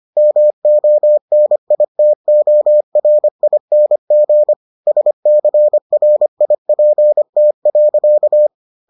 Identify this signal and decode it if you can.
Morse code, heard as a keyed tone, about 25 wpm, 600 Hz. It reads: MONITORING SCRIPT.